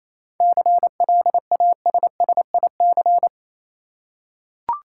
Morse code, heard as a keyed tone, about 28 wpm, 700 Hz. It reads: CLASSIC
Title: Seven Letter Words - 28wpm